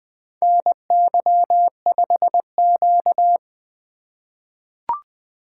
Morse code, heard as a keyed tone, about 20 wpm, 700 Hz. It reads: NY5Q
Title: USA Call Signs - 4 Characters - 2x Character Spacing - 20wpm